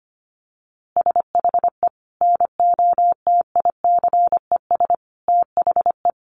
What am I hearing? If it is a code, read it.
Morse code, heard as a keyed tone, about 25 wpm, 700 Hz. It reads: SHE NOTICES THE
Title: Easy Phrases 25wpm